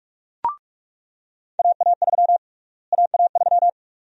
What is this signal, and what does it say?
Morse code, heard as a keyed tone, about 44 wpm, 700 Hz. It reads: AA3 AA3
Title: USA Call Sign Prefixes - Repeated 2x - 45wpm